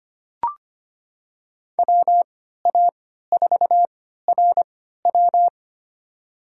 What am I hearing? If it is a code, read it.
Morse code, heard as a keyed tone, about 25 wpm, spaced as for 14 wpm, 700 Hz. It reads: WA4RW
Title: USA Call Signs - 5 Characters - 3x Character Spacing - 25wpm